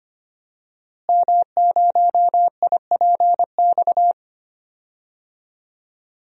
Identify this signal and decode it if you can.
Morse code, heard as a keyed tone, about 25 wpm, 700 Hz. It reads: M0IPX
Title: DXCC-100 Call Signs - 5 Characters - Repeated 2x - 25wpm